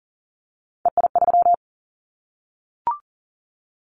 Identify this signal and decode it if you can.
Morse code, heard as a keyed tone, about 43 wpm, 700 Hz. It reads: EI3
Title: Callsign Prefixes for Top 50 DXCC - 40wpm